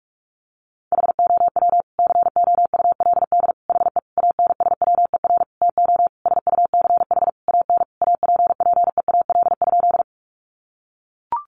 Morse code, heard as a keyed tone, about 44 wpm, 700 Hz. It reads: HOW COULD HE ANSWER TO SUCH AN APPEAL?
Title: Sentences from Top 1300 Words 45wpm